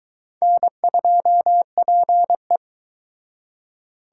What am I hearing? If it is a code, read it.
Morse code, heard as a keyed tone, about 23 wpm, 700 Hz. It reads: N2PE